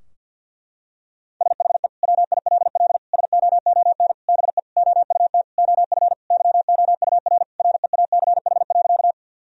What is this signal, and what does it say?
Morse code, heard as a keyed tone, about 52 wpm, 700 Hz. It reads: SHE WILL SOON BE OUT OF YOUR REACH.
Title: Sentences from Top 1000 Words - Review Part 2 - 50wpm